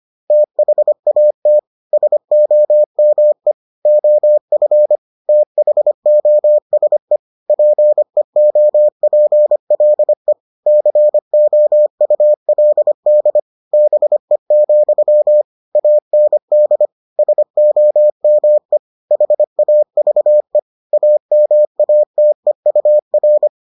Morse code, heard as a keyed tone, about 25 wpm, 600 Hz. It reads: THAT SOME OF THOSE PEOPLE COULD BE, AND SOME HAVE AMATEUR